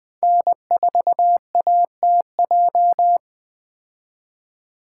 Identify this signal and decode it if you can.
Morse code, heard as a keyed tone, about 20 wpm, 700 Hz. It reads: N4ATJ
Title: 2023 POTA Activator Call Signs - 20wpm